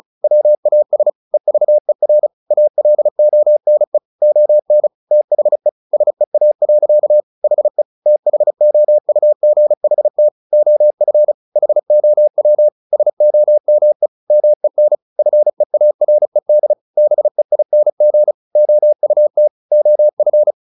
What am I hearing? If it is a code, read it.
Morse code, heard as a keyed tone, about 34 wpm, 600 Hz. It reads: WAS EVER ALONE ON THE SEA. HE THOUGHT OF HOW SOME MEN FEARED BEING OUT OF
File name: the old man and the sea / the old man and the sea 1 35wpm